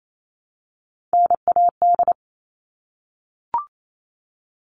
Morse code, heard as a keyed tone, about 28 wpm, 700 Hz. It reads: NAD
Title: All Cut Numbers - 3-Digit 28wpm